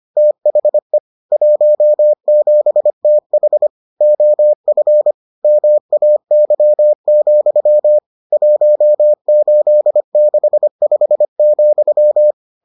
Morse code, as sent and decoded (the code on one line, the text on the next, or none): - .... / .---- --... - .... / --- ..-. / -- .- -.-- --..-- / .---- ---.. -.... ..... --..--
THE 17TH OF MAY, 1865,